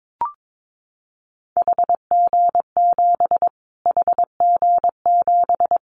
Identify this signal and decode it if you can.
Morse code, heard as a keyed tone, about 22 wpm, 700 Hz. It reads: HG7 HG7
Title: Callsign Prefixes for Top 25 DXCC - Repeated 3x - 22wpm